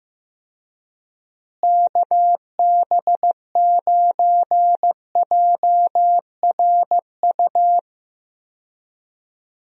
Morse code, heard as a keyed tone, about 15 wpm, 700 Hz. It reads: KB9JRU
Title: USA Call Signs - 6 Characters - Repeated 2x - 15wpm